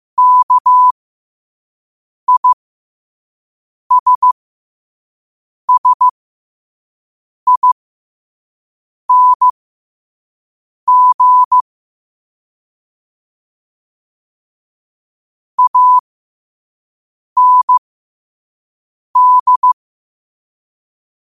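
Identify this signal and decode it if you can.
Morse code, heard as a keyed tone, about 15 wpm, spaced as for 5 wpm, 1000 Hz. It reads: KISSING AND
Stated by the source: Morse code sent at 5 wpm for learning.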